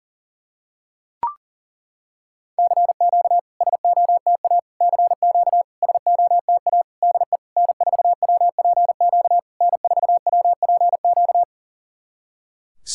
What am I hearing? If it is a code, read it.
Morse code, heard as a keyed tone, about 40 wpm, 700 Hz. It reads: CQ SOTA CQ SOTA DE N4WPQ N4WPQ
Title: SOTA QSO Practice 40wpm